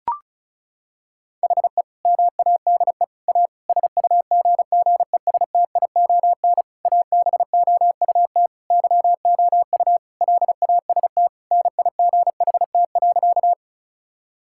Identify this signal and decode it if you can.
Morse code, heard as a keyed tone, about 36 wpm, 700 Hz. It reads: HE MADE A SUGGESTION ABOUT YOU LAST NIGHT.
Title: Sentences from Top 1600 Words 35wpm